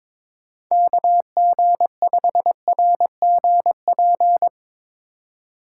Morse code, heard as a keyed tone, about 24 wpm, 700 Hz. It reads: KG5RGP
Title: Speed Racing - USA 2x3 Call Signs 22wpm